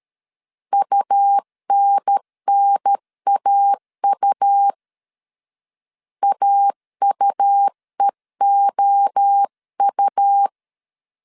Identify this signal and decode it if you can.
Morse code, heard as a keyed tone, about 13 wpm, 800 Hz. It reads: UNNAU AUEOU
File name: CW / DET / 1 3 1500C S A 12wpm 2905